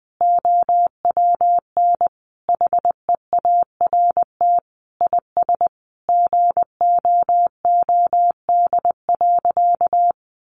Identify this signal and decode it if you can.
Morse code, heard as a keyed tone, about 20 wpm, 700 Hz. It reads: OWN HEART IS GOOD.
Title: Sentences from Top 1200 Words 20wpm